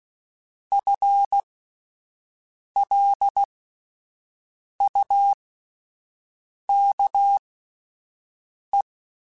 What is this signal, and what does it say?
Morse code, heard as a keyed tone, about 16 wpm, spaced as for 5 wpm, 800 Hz. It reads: FLUKE
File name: Spaced / FL Runs sp mp3s / FL Runs sp @10